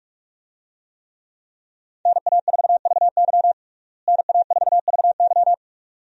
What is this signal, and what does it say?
Morse code, heard as a keyed tone, about 41 wpm, 700 Hz. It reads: NA4VY NA4VY
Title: USA Call Signs - 5 Characters - Repeated 2x - 45wpm